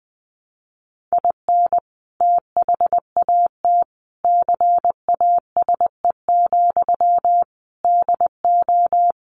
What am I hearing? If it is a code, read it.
Morse code, heard as a keyed tone, about 20 wpm, 700 Hz. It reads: IN THAT CASE, DO
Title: Sentences from Top 1300 Words 20wpm